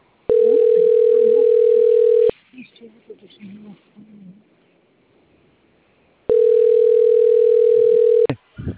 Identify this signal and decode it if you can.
Ringback tone (North America), 440 + 480 Hz.